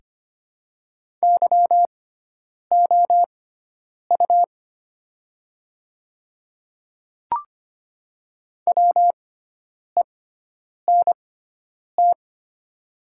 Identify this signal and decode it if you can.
Morse code, heard as a keyed tone, about 25 wpm, spaced as for 9 wpm, 700 Hz. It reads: YOU WENT
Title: 6x Character Spacing - Top 200 Words - 25wpm